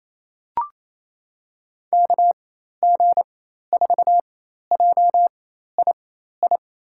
Morse code, heard as a keyed tone, about 28 wpm, spaced as for 13 wpm, 700 Hz. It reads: KG4JII
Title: USA Call Signs - 6 Characters - 4x Character Spacing - 28wpm